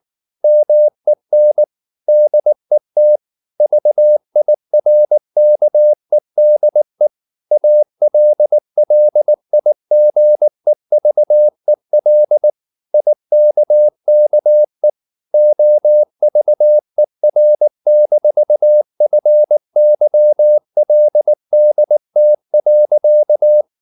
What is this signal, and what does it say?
Morse code, heard as a keyed tone, about 19 wpm, 600 Hz. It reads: MEN DET VIRKEDE ALLIGEVEL IKKE OVER-FYLDT.
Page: Ferie 19 wpm